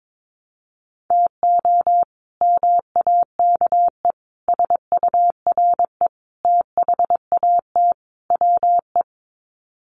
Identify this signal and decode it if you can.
Morse code, heard as a keyed tone, about 22 wpm, 700 Hz. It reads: TO MAKE SURE THAT WE
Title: Sets of 5 Words Encore 22wpm